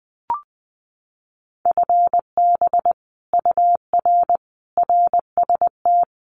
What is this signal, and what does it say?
Morse code, heard as a keyed tone, about 20 wpm, 700 Hz. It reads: FB UR RST
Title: Common QSO Phrases Encore 20wpm